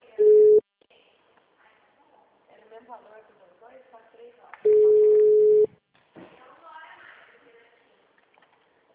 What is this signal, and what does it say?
Ringback tone (Europe, 425 Hz), with speech.
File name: out- -2002- - - .98748